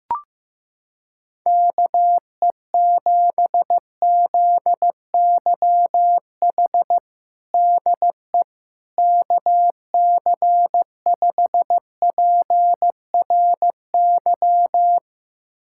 Morse code, heard as a keyed tone, about 15 wpm, 700 Hz. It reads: KE7ZYH DE KC5PRY
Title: USA Callsign DE Callsign - 6 Characters - 15wpm